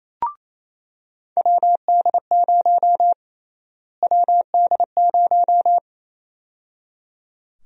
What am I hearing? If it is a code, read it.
Morse code, heard as a keyed tone, about 28 wpm, 700 Hz. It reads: WD0 WD0
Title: USA Call Sign Prefixes - Repeated 2x - 28wpm